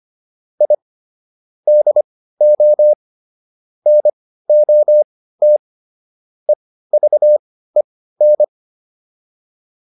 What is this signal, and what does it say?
Morse code, heard as a keyed tone, about 25 wpm, spaced as for 15 wpm, 600 Hz. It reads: I DO NOT EVEN